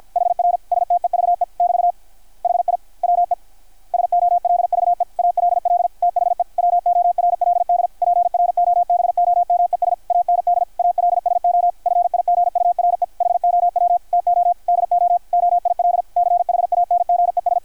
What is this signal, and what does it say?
Morse code, heard as a keyed tone, about 53 wpm, 700 Hz. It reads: HR STEVE <BT> HI WE SOLVE ALL THE WORLD PROBOMES AND ALSO FIGURE HOW TO DO OIL CHANGES
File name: archived / CRSnet